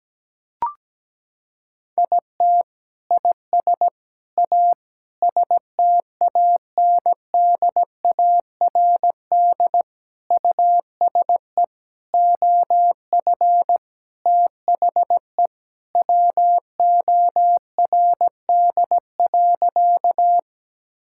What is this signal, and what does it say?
Morse code, heard as a keyed tone, about 17 wpm, 700 Hz. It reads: IT IS A STANDARD USE OF THE WORD.
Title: Sentences from Top 1000 Words - Review Part 6 - 17wpm